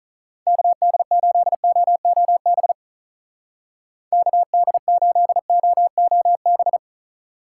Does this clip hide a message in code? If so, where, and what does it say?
Yes: Morse code from 0.5 to 6.8 s, heard as a keyed tone, about 45 wpm, 700 Hz. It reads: KD8OOB KD8OOB